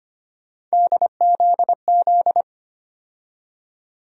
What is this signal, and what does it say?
Morse code, heard as a keyed tone, about 25 wpm, 700 Hz. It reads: DZZ
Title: USA Call Sign Postfixes - 3 Letters 25wpm